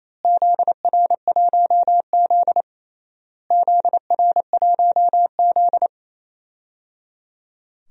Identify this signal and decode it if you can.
Morse code, heard as a keyed tone, about 28 wpm, 700 Hz. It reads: ZR1Z ZR1Z
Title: DXCC-100 Call Signs - 4 Characters - Repeated 3x - 28wpm